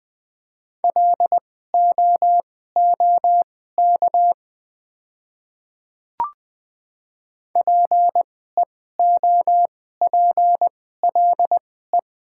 Morse code, heard as a keyed tone, about 20 wpm, spaced as for 14 wpm, 700 Hz. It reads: LOOK PEOPLE
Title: 2x Character Spacing - Top 100 Words - 20wpm